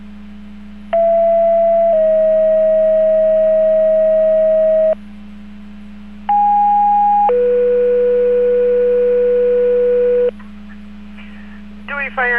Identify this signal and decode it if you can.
Two-tone page, heard as a one-second tone, then a three-second tone, about 670 then 640 Hz, about 820 then 490 Hz, with speech. It is heard from 0.9 to 10.3 s.